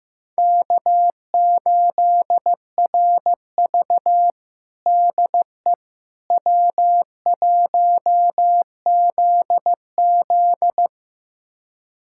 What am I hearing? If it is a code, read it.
Morse code, heard as a keyed tone, about 16 wpm, 700 Hz. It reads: K8RV DE W1ZZ
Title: USA Callsign DE Callsign - 4 Characters - 15wpm